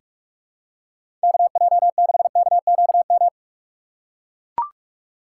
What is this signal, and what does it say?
Morse code, heard as a keyed tone, about 44 wpm, 700 Hz. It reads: KJ6KQM
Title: USA Call Signs - 6 Characters - 6x Character Spacing - 45wpm